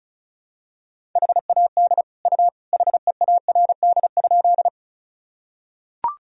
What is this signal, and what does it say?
Morse code, heard as a keyed tone, about 34 wpm, 700 Hz. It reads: HAD U HEARD?
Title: Sentences from Top 400 Words and Abbreviations 35wpm